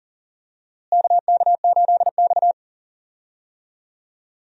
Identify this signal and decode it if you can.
Morse code, heard as a keyed tone, about 38 wpm, 700 Hz. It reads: KK8X
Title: USA Call Signs - 4 Characters - Repeated 2x - 40wpm